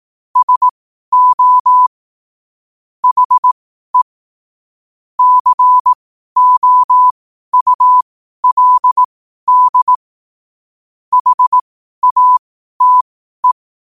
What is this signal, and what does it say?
Morse code, heard as a keyed tone, about 18 wpm, spaced as for 12 wpm, 1000 Hz. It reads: SO HE COULD HATE
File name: qotdfiles / 12wpm